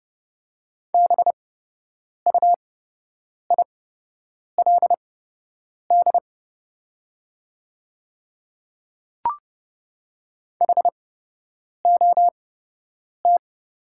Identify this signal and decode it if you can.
Morse code, heard as a keyed tone, about 29 wpm, spaced as for 8 wpm, 690 Hz. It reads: BUILD HOT
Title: 8x Character Spacing - Top 200 Words - 30wpm